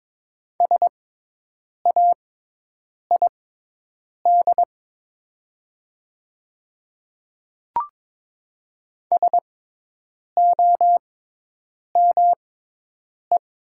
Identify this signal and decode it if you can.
Morse code, heard as a keyed tone, about 22 wpm, spaced as for 8 wpm, 700 Hz. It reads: SAID SOME
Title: 6x Character Spacing - Top 200 Words - 22wpm